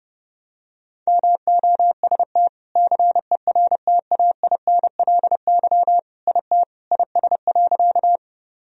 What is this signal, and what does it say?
Morse code, heard as a keyed tone, about 30 wpm, 700 Hz. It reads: MOST CERTAINLY IT IS.
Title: Sentences from Top 1000 Words - Review Part 1 - 30wpm